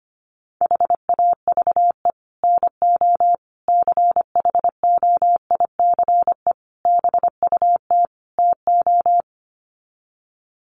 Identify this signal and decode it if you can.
Morse code, heard as a keyed tone, about 25 wpm, 700 Hz. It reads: HAVE NO CHOICE BUT TO
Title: Sets of 5 Words Encore 25wpm